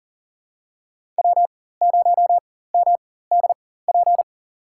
Morse code, heard as a keyed tone, about 40 wpm, spaced as for 19 wpm, 700 Hz. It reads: W0MDP
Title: USA Call Signs - 4x Character Spacing - 40wpm